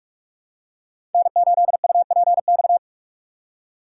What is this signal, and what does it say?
Morse code, heard as a keyed tone, about 45 wpm, 700 Hz. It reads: N8UPX
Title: USA Call Signs - 4x Character Spacing - 45wpm